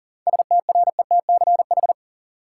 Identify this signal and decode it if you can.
Morse code, heard as a keyed tone, about 40 wpm, 700 Hz. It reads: STRETCH
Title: Seven Letter Words - 40wpm